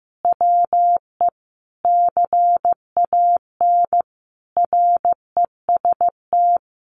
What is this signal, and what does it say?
Morse code, heard as a keyed tone, about 15 wpm, 700 Hz. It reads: WE CAN REST
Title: Sentences from Top 500 Words 15wpm